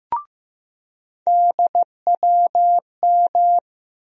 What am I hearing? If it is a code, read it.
Morse code, heard as a keyed tone, about 15 wpm, 700 Hz. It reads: DWM
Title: USA Call Sign Postfixes - 3 Letters 15wpm